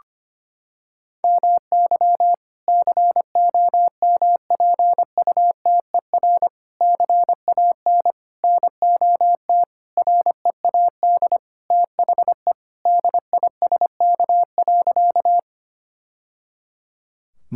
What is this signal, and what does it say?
Morse code, heard as a keyed tone, about 25 wpm, 700 Hz. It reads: MY COMPUTER CAN NOT READ THE DISK.
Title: Sentences from Top 1000 Words - Review Part 6 - 25wpm